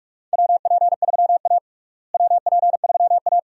Morse code, heard as a keyed tone, about 46 wpm, 700 Hz. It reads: WP3A WP3A
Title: USA Call Signs - 4 Characters - Repeated 2x - 45wpm